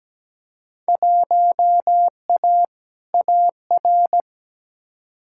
Morse code, heard as a keyed tone, about 17 wpm, 700 Hz. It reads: 1A AR
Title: Field Day Exchanges - 17wpm